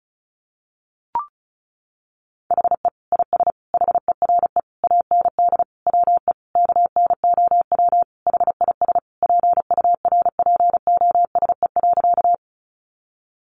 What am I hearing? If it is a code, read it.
Morse code, heard as a keyed tone, about 34 wpm, 700 Hz. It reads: HE IS HERE AND WE KNOW HIS PURPOSE.
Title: Sentences from Top 800 Words 35wpm